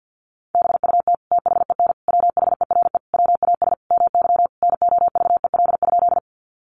Morse code, heard as a keyed tone, about 50 wpm, 700 Hz. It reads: BUT THEN WHERE WAS MY NOVEL?
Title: Sentences from Top 1300 Words 50wpm